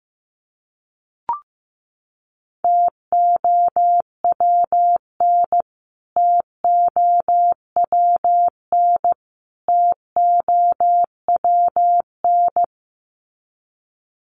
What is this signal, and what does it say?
Morse code, heard as a keyed tone, about 15 wpm, 700 Hz. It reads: TOWN TOWN TOWN
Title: Top 500 Words Repeated 15wpm